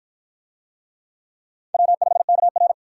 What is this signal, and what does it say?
Morse code, heard as a keyed tone, about 48 wpm, 700 Hz. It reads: W5KR